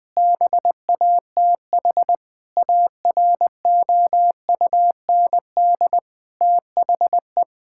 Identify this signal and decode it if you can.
Morse code, heard as a keyed tone, about 20 wpm, 700 Hz. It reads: BATH AROUND THE
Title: Sentences from Top 1600 Words 20wpm